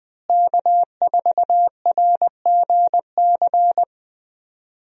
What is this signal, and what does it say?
Morse code, heard as a keyed tone, about 20 wpm, 700 Hz. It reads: K4RGC
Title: USA Call Signs - 5 Characters - Repeated 2x - 20wpm